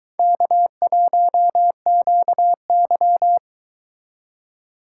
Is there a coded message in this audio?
Yes: Morse code from 0.2 to 3.4 s, heard as a keyed tone, about 22 wpm, 700 Hz. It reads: K1QY